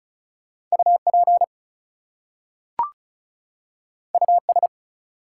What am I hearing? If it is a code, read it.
Morse code, heard as a keyed tone, about 36 wpm, 700 Hz. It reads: UP US
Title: Two Letter Words - 35wpm